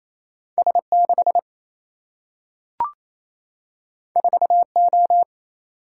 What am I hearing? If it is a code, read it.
Morse code, heard as a keyed tone, about 28 wpm, 700 Hz. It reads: S 6 4 O
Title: Single-Letter Single-Number 28wpm